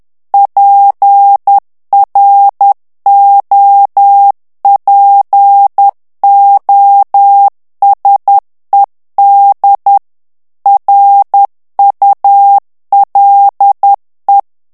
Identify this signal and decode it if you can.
Morse code, heard as a keyed tone, about 10 wpm, 800 Hz. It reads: PROPOSED RULE
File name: ANGLAIS texte clair 3 vitesse 10 mots minute